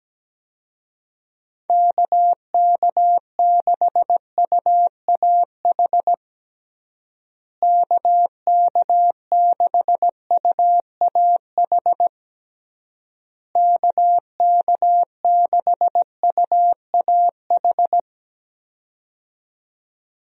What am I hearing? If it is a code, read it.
Morse code, heard as a keyed tone, about 17 wpm, 700 Hz. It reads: KK6UAH KK6UAH KK6UAH
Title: USA Call Signs - 6 Characters - Repeated 3x - 17wpm